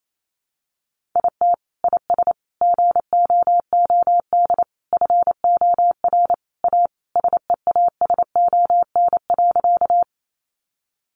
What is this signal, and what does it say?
Morse code, heard as a keyed tone, about 28 wpm, 700 Hz. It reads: IT IS GOOD FOR A SEASON.
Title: Sentences from Top 1400 Words 28wpm